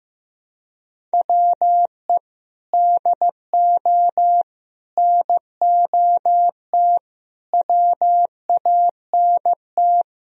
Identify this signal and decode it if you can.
Morse code, heard as a keyed tone, about 15 wpm, 700 Hz. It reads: WE DO NOT WANT
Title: Speed-Racing - Sets of 4 Words 15wpm